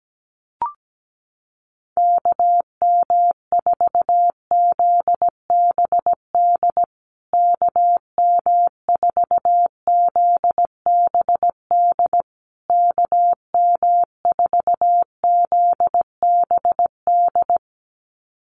Morse code, heard as a keyed tone, about 17 wpm, 700 Hz. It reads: KM4ZBD KM4ZBD KM4ZBD
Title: USA Call Signs Repeated 3x 17wpm